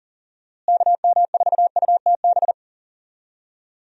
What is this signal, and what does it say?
Morse code, heard as a keyed tone, about 40 wpm, 700 Hz. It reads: KM4UTB
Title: USA Call Signs - 6 Characters - Repeated 2x - 40wpm